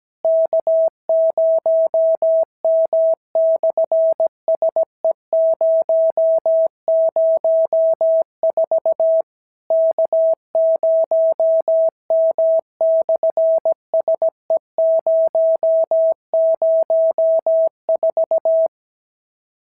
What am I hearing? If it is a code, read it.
Morse code, heard as a keyed tone, about 17 wpm, 650 Hz. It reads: K0M/SE004 K0M/SE004
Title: SOTA QSO Practice 17wpm